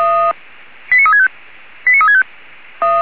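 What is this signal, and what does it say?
Five-tone selective call (ZVEI), heard as quick runs of five tones, tones of about 80 ms. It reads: 89365 89365
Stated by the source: ZVEI five-tone call received by SDR.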